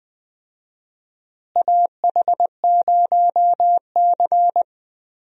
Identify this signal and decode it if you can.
Morse code, heard as a keyed tone, about 20 wpm, 700 Hz. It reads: AH0C
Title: USA Call Signs - 4 Characters - Repeated 2x - 20wpm